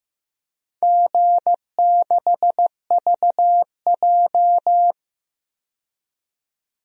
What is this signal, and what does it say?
Morse code, heard as a keyed tone, about 15 wpm, 700 Hz. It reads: G6VJ
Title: DXCC-100 Call Signs - 4 Characters - 15wpm